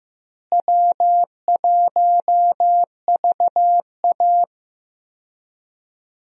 Morse code, heard as a keyed tone, about 15 wpm, 700 Hz. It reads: W1VA
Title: USA Call Signs - 4 Characters - Repeated 3x - 15wpm